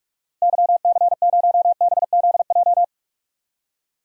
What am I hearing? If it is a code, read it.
Morse code, heard as a keyed tone, about 43 wpm, 700 Hz. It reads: YC0BZJ
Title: DXCC-100 Call Signs - 6 Characters - 45wpm